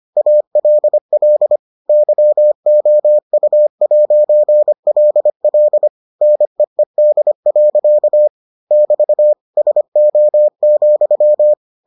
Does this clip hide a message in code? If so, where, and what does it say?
Yes: Morse code from 0.2 to 11.5 s, heard as a keyed tone, about 25 wpm, 600 Hz. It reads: ALL YOU'LL NEED. <BT> SO,